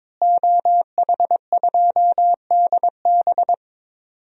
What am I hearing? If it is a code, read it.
Morse code, heard as a keyed tone, about 21 wpm, 700 Hz. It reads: OH2DB
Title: DXCC-100 Call Signs - 5 Characters - Repeated 2x - 22wpm